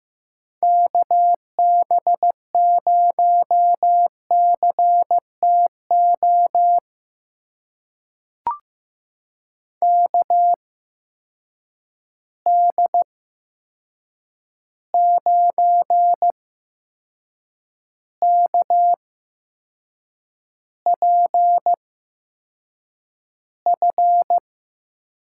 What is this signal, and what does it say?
Morse code, heard as a keyed tone, about 15 wpm, spaced as for 6 wpm, 700 Hz. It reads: KB0CTO KD9KPF
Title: USA Call Signs - 6 Characters - 8x Character Spacing - 15wpm